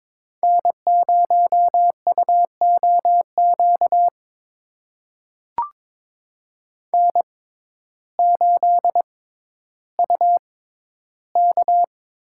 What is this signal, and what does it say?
Morse code, heard as a keyed tone, about 22 wpm, spaced as for 12 wpm, 700 Hz. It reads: N0UOQ N8UK
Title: USA Call Signs - 5 Characters - 6x Character Spacing - 22wpm